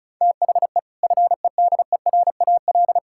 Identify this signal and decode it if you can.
Morse code, heard as a keyed tone, about 36 wpm, 700 Hz. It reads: THE FEDERAL
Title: Sentences from Top 1000 Words - Review Part 5 - 35wpm